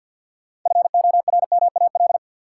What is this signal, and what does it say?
Morse code, heard as a keyed tone, about 52 wpm, 700 Hz. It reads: FORMAL
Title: Sentences from Top 1300 Words - New Word Drill - 50wpm